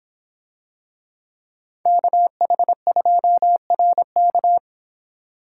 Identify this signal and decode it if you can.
Morse code, heard as a keyed tone, about 26 wpm, 700 Hz. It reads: KH2RK